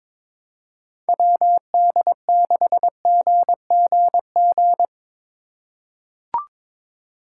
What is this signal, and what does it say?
Morse code, heard as a keyed tone, about 21 wpm, 700 Hz. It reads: WD6GGG
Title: USA Call Signs - 6 Characters - 4x Character Spacing - 22wpm